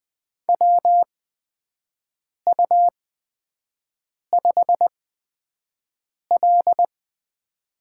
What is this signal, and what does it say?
Morse code, heard as a keyed tone, about 20 wpm, spaced as for 5 wpm, 700 Hz. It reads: WU5L
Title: USA Call Signs - 8x Character Spacing - 20wpm